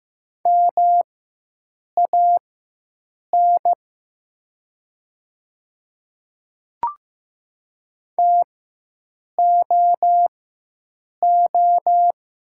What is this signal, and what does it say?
Morse code, heard as a keyed tone, about 15 wpm, spaced as for 7 wpm, 690 Hz. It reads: MAN TOO
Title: 4x Character Spacing - Top 200 Words - 15wpm